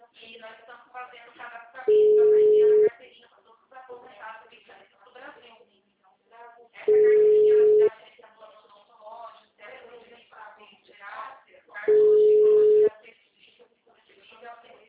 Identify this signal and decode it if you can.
Ringback tone (Europe, 425 Hz), with speech.
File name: out- -2025- - - .61173